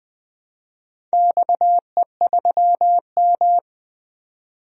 Morse code, heard as a keyed tone, about 20 wpm, 700 Hz. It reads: XE3M
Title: DXCC-100 Call Signs - 4 Characters - 20wpm